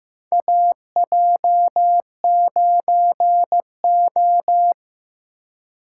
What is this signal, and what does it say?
Morse code, heard as a keyed tone, about 15 wpm, 700 Hz. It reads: AJ9O
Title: USA Call Signs - 4 Characters - 3x Character Spacing - 15wpm